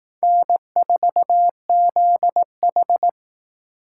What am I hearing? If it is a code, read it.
Morse code, heard as a keyed tone, about 18 wpm, 700 Hz. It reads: N4ZH
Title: Speed Racing - USA Call Signs 15wpm